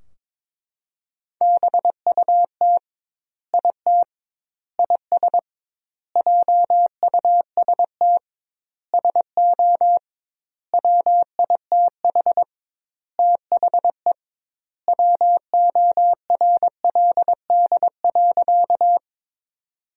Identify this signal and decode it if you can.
Morse code, heard as a keyed tone, about 21 wpm, 700 Hz. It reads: BUT IT IS JUST SO WITH THE WORLD.
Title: 2x Word-Spacing - Sentences from Top 200 Words - 22wpm